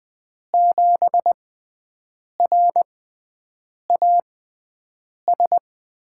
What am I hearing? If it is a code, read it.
Morse code, heard as a keyed tone, about 20 wpm, spaced as for 7 wpm, 700 Hz. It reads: 7RAS
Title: USA Call Signs - 6x Character Spacing - 20wpm